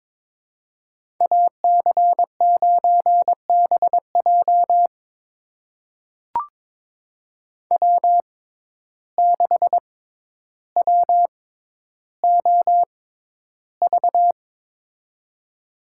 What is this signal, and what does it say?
Morse code, heard as a keyed tone, about 23 wpm, spaced as for 11 wpm, 700 Hz. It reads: AC9BJ W6WOV
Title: USA Call Signs - 5 Characters - 6x Character Spacing - 22wpm